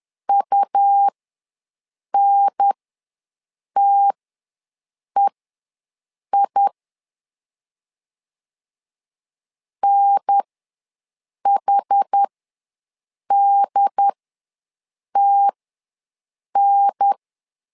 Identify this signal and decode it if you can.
Morse code, heard as a keyed tone, about 10 wpm, spaced as for 6 wpm, 800 Hz. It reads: UNTEI NHDTN
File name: LEZIONI / 2 UHDN / 2 UHDN ws 10wpm 2044